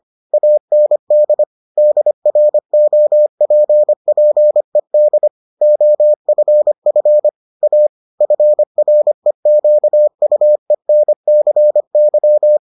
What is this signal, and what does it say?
Morse code, heard as a keyed tone, about 25 wpm, 600 Hz. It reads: AND DROPPED OFF A FREQUENCY